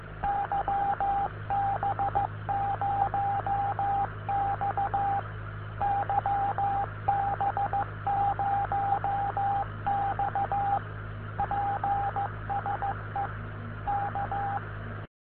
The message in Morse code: YB0X YB0X PSE K